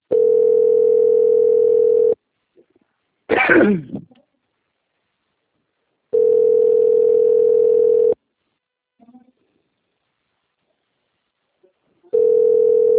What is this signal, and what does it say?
Ringback tone (North America), 440 + 480 Hz.